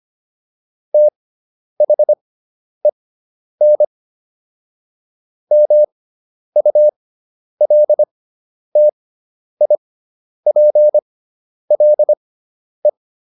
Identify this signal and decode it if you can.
Morse code, heard as a keyed tone, about 25 wpm, spaced as for 10 wpm, 600 Hz. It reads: THEN MULTIPLE